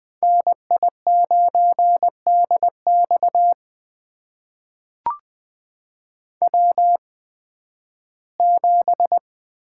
Morse code, heard as a keyed tone, about 20 wpm, spaced as for 13 wpm, 700 Hz. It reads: NI9DX W7
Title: USA Call Signs - 8x Character Spacing - 20wpm